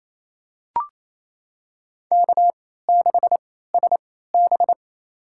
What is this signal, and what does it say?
Morse code, heard as a keyed tone, about 28 wpm, spaced as for 16 wpm, 700 Hz. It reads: K6SB
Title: USA Call Signs - 4 Characters - 3x Character Spacing - 28wpm